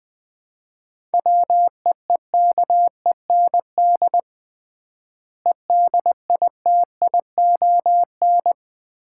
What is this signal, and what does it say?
Morse code, heard as a keyed tone, about 21 wpm, 700 Hz. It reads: WEEKEND EDITION
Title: 3x Word-Spacing - Sets of 5 Words - 20wpm